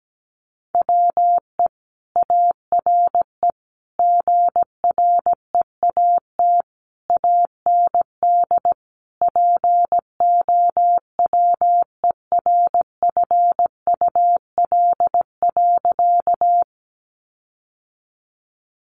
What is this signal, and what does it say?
Morse code, heard as a keyed tone, about 17 wpm, 700 Hz. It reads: WE ARE GREAT AND POWERFUL.
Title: Sentences from Top 1300 Words 17wpm